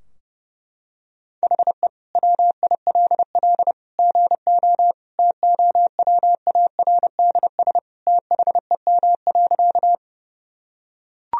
Morse code, heard as a keyed tone, about 30 wpm, 700 Hz. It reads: HE WILL GO TOWARDS THEM.
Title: Sentences from Top 1000 Words - Review Part 2 - 30wpm